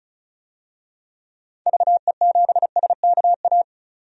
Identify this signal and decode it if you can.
Morse code, heard as a keyed tone, about 36 wpm, 700 Hz. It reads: VE7SKA
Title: DXCC-100 Call Signs - 6 Characters - 35wpm